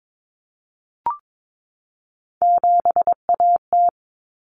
Morse code, heard as a keyed tone, about 22 wpm, 700 Hz. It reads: 7AT
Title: Common Cut Numbers - 3-Digit 22wpm